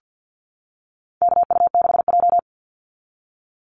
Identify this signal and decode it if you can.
Morse code, heard as a keyed tone, about 50 wpm, 700 Hz. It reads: KU6J
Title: NA SOTA Activator Callsigns 50wpm